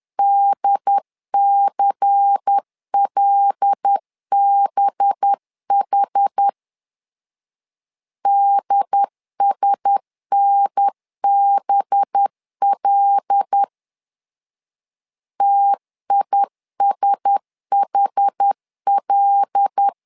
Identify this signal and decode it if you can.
Morse code, heard as a keyed tone, about 10 wpm, 800 Hz. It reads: DCLBH DSNBL TISHL
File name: LEZIONI / 5 LBVC / 5 LBVC 10wpm 2349